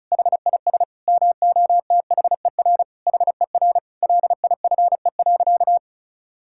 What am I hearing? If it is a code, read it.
Morse code, heard as a keyed tone, about 36 wpm, 700 Hz. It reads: HIS MOTHER HER LIFE.
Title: Sentences from Top 1000 Words - Review Part 5 - 35wpm